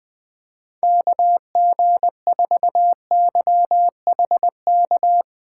Morse code, heard as a keyed tone, about 20 wpm, 700 Hz. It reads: KG4YHK
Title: USA Call Signs - 2x Character Spacing - 20wpm